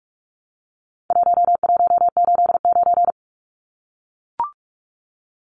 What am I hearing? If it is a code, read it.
Morse code, heard as a keyed tone, about 44 wpm, 700 Hz. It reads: 1189
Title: Numbers - 4-Digit 45wpm